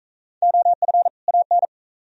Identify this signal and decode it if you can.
Morse code, heard as a keyed tone, about 40 wpm, 700 Hz. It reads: OF AN